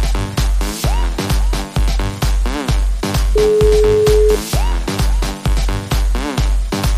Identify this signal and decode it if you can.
Ringback tone (Europe, 425 Hz), with noise.